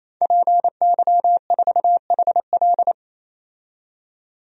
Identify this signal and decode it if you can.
Morse code, heard as a keyed tone, about 28 wpm, 700 Hz. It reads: PY4HL
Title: DXCC-100 Call Signs - 5 Characters - 28wpm